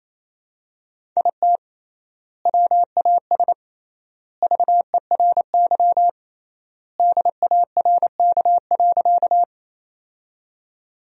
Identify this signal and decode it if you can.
Morse code, heard as a keyed tone, about 28 wpm, 700 Hz. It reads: IT WAS VERY DARK.
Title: 3x Word-Spacing - Sentences from Top 500 Words - 28wpm